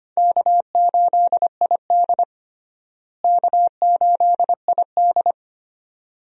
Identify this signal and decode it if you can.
Morse code, heard as a keyed tone, about 25 wpm, 700 Hz. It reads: K8ID K8ID
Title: USA Call Signs - 4 Characters - Repeated 2x - 25wpm